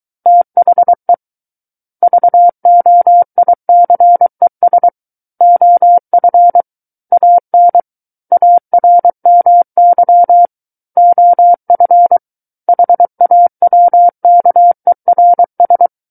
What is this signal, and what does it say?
Morse code, heard as a keyed tone, about 23 wpm, 700 Hz. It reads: THE VOICES OF AN ARMY OF HAWKERS
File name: Book One / 14-Chapter 14 - 23 WPM